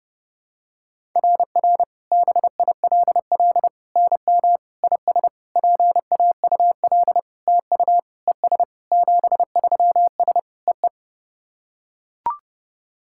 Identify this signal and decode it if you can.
Morse code, heard as a keyed tone, about 30 wpm, 700 Hz. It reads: RR BILL NM IS PAUL TU ES 73S EE
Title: SOTA QSO Practice 30wpm